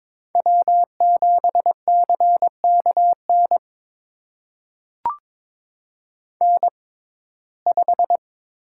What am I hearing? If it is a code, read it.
Morse code, heard as a keyed tone, about 22 wpm, spaced as for 16 wpm, 700 Hz. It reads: W7CKN N5
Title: USA Call Signs - 5 Characters - 6x Character Spacing - 22wpm